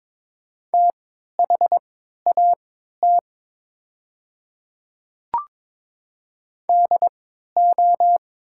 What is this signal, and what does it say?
Morse code, heard as a keyed tone, about 22 wpm, spaced as for 13 wpm, 700 Hz. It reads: THAT DO
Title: 3x Character Spacing - Top 100 Words - 22wpm